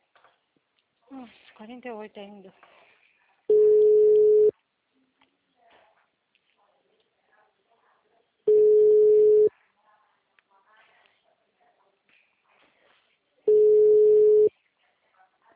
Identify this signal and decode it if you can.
Ringback tone (Europe, 425 Hz), with speech.